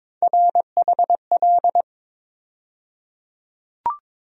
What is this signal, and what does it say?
Morse code, heard as a keyed tone, about 22 wpm, 700 Hz. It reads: RHL
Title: USA Call Sign Postfixes - 3 Letters 22wpm